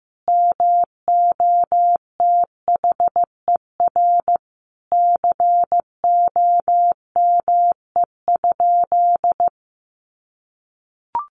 Morse code, heard as a keyed tone, about 14 wpm, 700 Hz. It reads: MOTHER COME?
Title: Sentences from Top 200 Words 15wpm